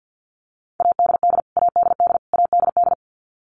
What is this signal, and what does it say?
Morse code, heard as a keyed tone, about 50 wpm, 700 Hz. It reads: ADD ADD ADD
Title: Top 300 Words Repeated 50wpm